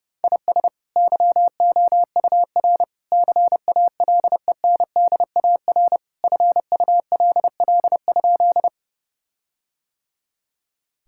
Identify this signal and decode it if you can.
Morse code, heard as a keyed tone, about 32 wpm, 700 Hz. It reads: IS YOUR CALENDAR FULL?
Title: Sentences from Top 1600 Words 30wpm